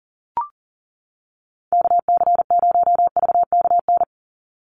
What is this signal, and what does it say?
Morse code, heard as a keyed tone, about 40 wpm, 700 Hz. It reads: KC0VKN
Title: Call Signs - 40wpm